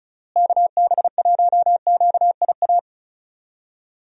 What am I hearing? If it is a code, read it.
Morse code, heard as a keyed tone, about 34 wpm, 700 Hz. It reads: KB1QIA
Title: USA Call Signs - 6 Characters - Repeated 3x - 35wpm